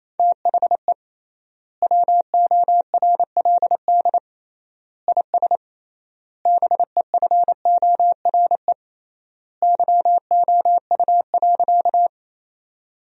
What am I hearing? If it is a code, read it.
Morse code, heard as a keyed tone, about 28 wpm, 700 Hz. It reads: THE WORLD IS BEFORE YOU.
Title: 3x Word-Spacing - Sentences from Top 200 Words - 28wpm